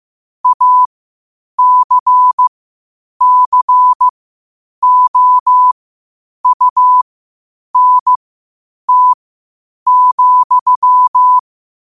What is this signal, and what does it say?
Morse code, heard as a keyed tone, about 15 wpm, spaced as for 8 wpm, 1000 Hz. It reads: ACCOUNT,
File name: qotdfiles / 8wpm